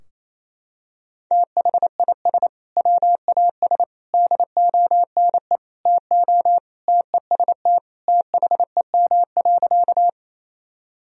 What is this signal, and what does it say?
Morse code, heard as a keyed tone, about 28 wpm, 700 Hz. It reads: THIS WAS DONE TO TEST THEM.
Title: Sentences from Top 1000 Words - Review Part 3 - 28wpm